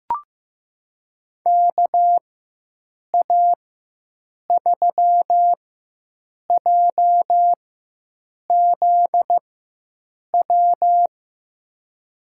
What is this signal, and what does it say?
Morse code, heard as a keyed tone, about 15 wpm, spaced as for 7 wpm, 700 Hz. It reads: KA3JZW
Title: USA Call Signs - 6 Characters - 4x Character Spacing - 15wpm